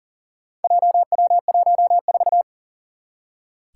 Morse code, heard as a keyed tone, about 40 wpm, 700 Hz. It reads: JW14
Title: Four Challenging Letter-Number Combinations 40wpm